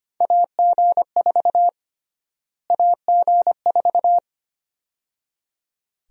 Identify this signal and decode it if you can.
Morse code, heard as a keyed tone, about 25 wpm, 700 Hz. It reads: AG4 AG4
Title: USA Call Sign Prefixes - Repeated 2x - 25wpm